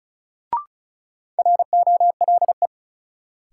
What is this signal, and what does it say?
Morse code, heard as a keyed tone, about 36 wpm, 700 Hz. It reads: ROLE
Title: Sets of 3 Words - New Word Drill - 35wpm